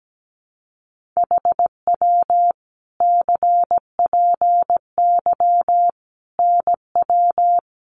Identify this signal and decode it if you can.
Morse code, heard as a keyed tone, about 17 wpm, 700 Hz. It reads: HW CPY NW
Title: Common QSO Phrases 17wpm